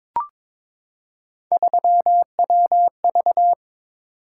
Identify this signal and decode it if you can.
Morse code, heard as a keyed tone, about 21 wpm, 700 Hz. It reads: 3WV
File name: practice / 22wpm / Three Letter-Number Combinations - 22wpm